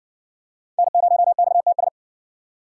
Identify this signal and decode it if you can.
Morse code, heard as a keyed tone, about 59 wpm, 700 Hz. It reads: N0XTS